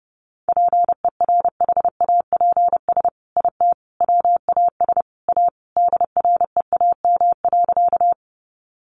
Morse code, heard as a keyed tone, about 28 wpm, 700 Hz. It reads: PERHAPS IT WAS A DREAM.
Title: Sentences from Top 1500 Words 30wpm